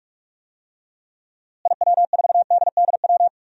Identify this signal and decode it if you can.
Morse code, heard as a keyed tone, about 43 wpm, 700 Hz. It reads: IW4DDW
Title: DXCC-100 Call Signs - 6 Characters - Repeated 2x - 45wpm